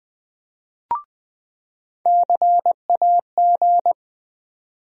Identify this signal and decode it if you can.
Morse code, heard as a keyed tone, about 21 wpm, 700 Hz. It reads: CAG
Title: USA Call Sign Postfixes - 3 Letters 20wpm